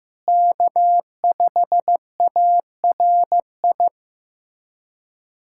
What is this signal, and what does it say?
Morse code, heard as a keyed tone, about 15 wpm, 700 Hz. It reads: K5ARI
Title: USA Callsign DE Callsign 15wpm